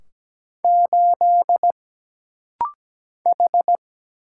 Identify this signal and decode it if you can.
Morse code, heard as a keyed tone, about 17 wpm, 700 Hz. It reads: 8 H
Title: Single Letter-Number - Rapid-Fire 17wpm